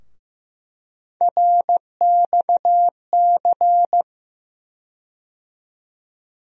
Morse code, heard as a keyed tone, about 15 wpm, 700 Hz. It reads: RXC
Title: USA Call Sign Postfixes - 3 Letters 15wpm